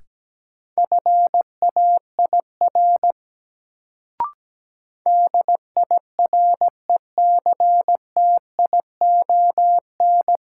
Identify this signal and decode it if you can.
Morse code, heard as a keyed tone, about 17 wpm, 700 Hz. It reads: FAIR DIRECTION
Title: Sentences from Top 1000 Words - Review Part 5 - New Word Drill - 17wpm